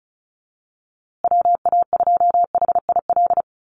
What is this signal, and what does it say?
Morse code, heard as a keyed tone, about 34 wpm, 700 Hz. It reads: WA2HIL
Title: USA Call Signs Repeated 3x 35wpm